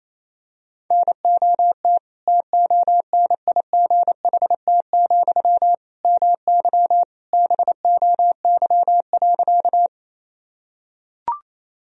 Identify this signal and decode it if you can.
Morse code, heard as a keyed tone, about 28 wpm, 700 Hz. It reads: NOT TONIGHT, MY BOY.
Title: Sentences from Top 1000 Words - Review Part 3 - 28wpm